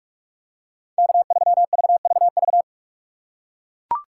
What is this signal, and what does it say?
Morse code, heard as a keyed tone, about 42 wpm, 700 Hz. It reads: K3VVV
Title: USA Call Signs - 5 Characters - 4x Character Spacing - 45wpm